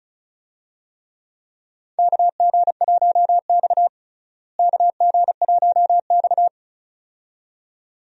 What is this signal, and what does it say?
Morse code, heard as a keyed tone, about 36 wpm, 700 Hz. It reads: KG1X KG1X
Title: USA Call Signs - 4 Characters - Repeated 2x - 35wpm